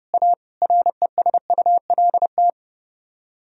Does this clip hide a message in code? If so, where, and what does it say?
Yes: Morse code from 0.1 to 2.5 s, heard as a keyed tone, about 28 wpm, 700 Hz. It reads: A RESULT